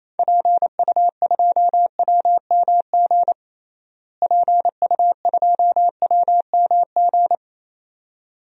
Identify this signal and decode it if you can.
Morse code, heard as a keyed tone, about 28 wpm, 700 Hz. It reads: PU2WMG PU2WMG
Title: DXCC-100 Call Signs - 6 Characters - Repeated 2x - 28wpm